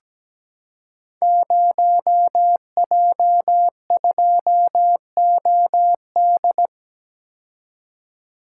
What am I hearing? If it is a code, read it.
Morse code, heard as a keyed tone, about 17 wpm, 700 Hz. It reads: 0J2OD
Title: Code Groups - 5 Letter:Numbers - 17wpm